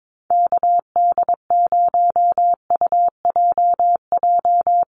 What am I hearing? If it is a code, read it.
Morse code, heard as a keyed tone, about 22 wpm, 700 Hz. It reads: KD0UJJ
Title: USA Call Signs 22wpm